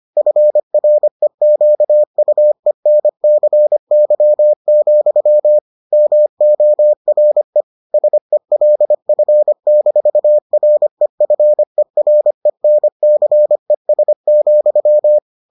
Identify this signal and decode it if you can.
Morse code, heard as a keyed tone, about 25 wpm, 600 Hz. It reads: FREQUENCY, MORE SELF-REFERENCES,